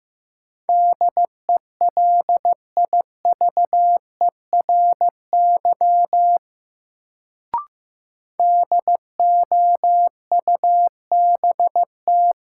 Morse code, heard as a keyed tone, about 15 wpm, 700 Hz. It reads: DELIVERY DOUBT
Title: Sentences from Top 1500 Words - New Word Drill - 15wpm